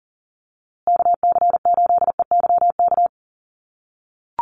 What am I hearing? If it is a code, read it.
Morse code, heard as a keyed tone, about 40 wpm, 700 Hz. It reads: KC8EYK
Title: USA Call Signs Repeated 2x 40wpm